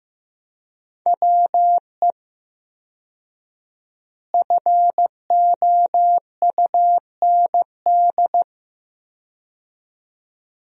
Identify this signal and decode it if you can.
Morse code, heard as a keyed tone, about 15 wpm, 700 Hz. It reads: WE FOUND
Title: 4x Word-Spacing - Sentences from Top 300 Words - 15wpm